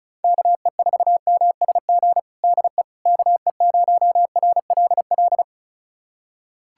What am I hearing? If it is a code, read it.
Morse code, heard as a keyed tone, about 36 wpm, 700 Hz. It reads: KE4MSG DE KE0RLL
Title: USA Callsign DE Callsign - 6 Characters - 35wpm